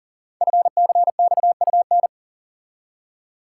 Morse code, heard as a keyed tone, about 38 wpm, 700 Hz. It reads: FCXUN
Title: Code Groups - 5 Letters - 40wpm